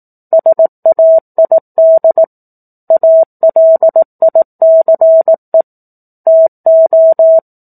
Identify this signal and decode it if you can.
Morse code, heard as a keyed tone, about 18 wpm, 650 Hz. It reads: SAID ALICE TO
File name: 18 WPM